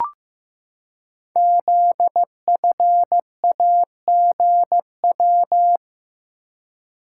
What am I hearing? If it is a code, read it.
Morse code, heard as a keyed tone, about 15 wpm, 700 Hz. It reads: ZFAGW
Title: Code Groups - 5 Letters - 15wpm